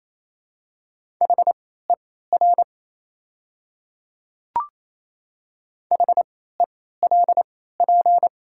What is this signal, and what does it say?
Morse code, heard as a keyed tone, about 28 wpm, spaced as for 16 wpm, 700 Hz. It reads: HER HELP
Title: 3x Character Spacing - Top 500 Words - 28wpm